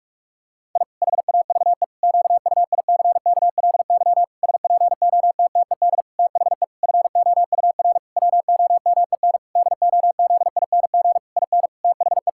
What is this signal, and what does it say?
Morse code, heard as a keyed tone, about 46 wpm, 700 Hz. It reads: I HAVE QUICKLY SPOTTED THE FOUR WOMEN DOZING IN THE
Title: Pangram Challenge v2 - 45wpm